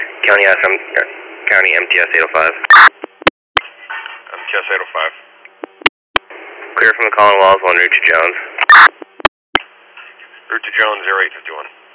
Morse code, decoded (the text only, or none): none